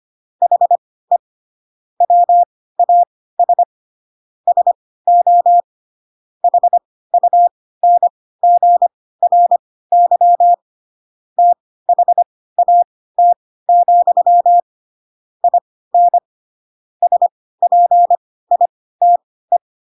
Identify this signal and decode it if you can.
Morse code, heard as a keyed tone, about 25 wpm, spaced as for 16 wpm, 700 Hz. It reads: HE WAS SO HUNGRY THAT, IN SPITE
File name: Pinocchio / Pinocchio ch6 10 25 ik1hge